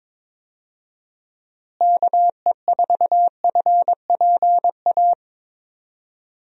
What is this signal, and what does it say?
Morse code, heard as a keyed tone, about 22 wpm, 700 Hz. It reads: KE4FPA
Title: USA Call Signs - 6 Characters - Repeated 3x - 22wpm